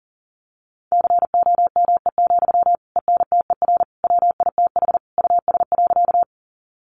Morse code, heard as a keyed tone, about 40 wpm, 700 Hz. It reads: COME, ENTER WITH US.
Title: Sentences from Top 1300 Words 40wpm